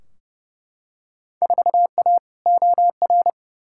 Morse code, heard as a keyed tone, about 30 wpm, 700 Hz. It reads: 4A OR
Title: Field Day Exchanges Encore - 30wpm